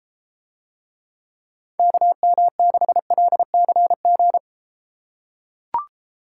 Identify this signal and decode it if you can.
Morse code, heard as a keyed tone, about 32 wpm, 700 Hz. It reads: KM6LCG